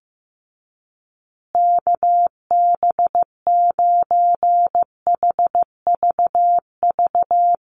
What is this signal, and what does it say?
Morse code, heard as a keyed tone, about 15 wpm, 700 Hz. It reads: KB9HVV
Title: USA Call Signs Repeated 3x 15wpm